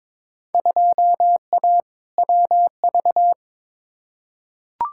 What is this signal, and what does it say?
Morse code, heard as a keyed tone, about 22 wpm, 700 Hz. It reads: 2A WV
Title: Field Day Exchanges Encore - 22wpm